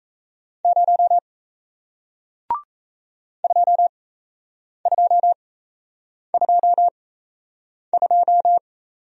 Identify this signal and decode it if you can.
Morse code, heard as a keyed tone, about 38 wpm, spaced as for 8 wpm, 700 Hz. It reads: 0 2222